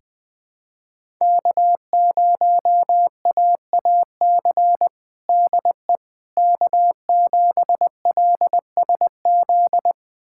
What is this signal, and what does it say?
Morse code, heard as a keyed tone, about 20 wpm, 700 Hz. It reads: K0AAC DE K7LSZ
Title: USA Callsign DE Callsign - 5 Characters - 20wpm